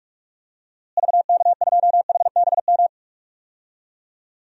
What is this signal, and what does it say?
Morse code, heard as a keyed tone, about 41 wpm, 700 Hz. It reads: VK2HBM
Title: DXCC-100 Call Signs - 6 Characters - Repeated 3x - 45wpm